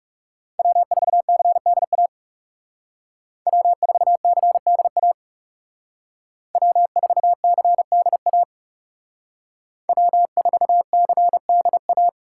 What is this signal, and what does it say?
Morse code, heard as a keyed tone, about 35 wpm, 700 Hz. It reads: W4CDA W4CDA W4CDA W4CDA
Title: Speed Racing - USA 1x3 Call Signs 30wpm